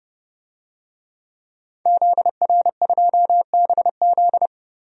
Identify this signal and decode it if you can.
Morse code, heard as a keyed tone, about 29 wpm, 700 Hz. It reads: ZR2BZ
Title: DXCC-100 Call Signs - 5 Characters - 30wpm